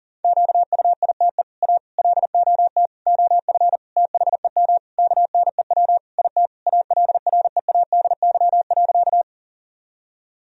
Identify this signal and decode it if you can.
Morse code, heard as a keyed tone, about 40 wpm, 700 Hz. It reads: QUITE A LOT OF THEM KNEW IT ALREADY.
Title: Sentences from Top 1000 Words - Review Part 3 - 40wpm